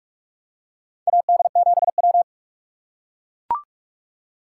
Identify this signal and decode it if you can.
Morse code, heard as a keyed tone, about 41 wpm, 700 Hz. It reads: AD7W
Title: USA Call Signs - 4 Characters - 4x Character Spacing - 45wpm